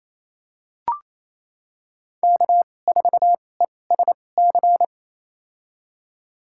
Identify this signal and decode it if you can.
Morse code, heard as a keyed tone, about 28 wpm, spaced as for 20 wpm, 700 Hz. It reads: K4ESC
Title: USA Call Signs - 5 Characters - 2x Character Spacing - 28wpm